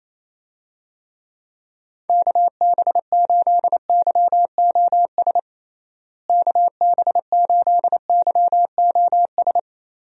Morse code, heard as a keyed tone, about 28 wpm, 700 Hz. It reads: KB8YOS KB8YOS
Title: USA Call Signs - 6 Characters - Repeated 2x - 28wpm